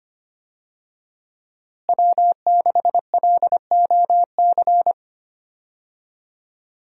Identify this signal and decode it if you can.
Morse code, heard as a keyed tone, about 25 wpm, 700 Hz. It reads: W6LOC
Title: USA Call Signs - 5 Characters - Repeated 2x - 25wpm